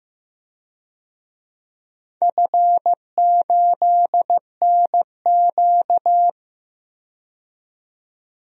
Morse code, heard as a keyed tone, about 15 wpm, 700 Hz. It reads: F8NQ
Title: DXCC-100 Call Signs - 4 Characters - 15wpm